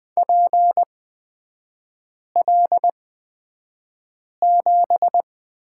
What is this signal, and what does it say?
Morse code, heard as a keyed tone, about 20 wpm, spaced as for 5 wpm, 700 Hz. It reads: PL7
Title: 255 Lesson 41e - Period - Three Characters 20wpm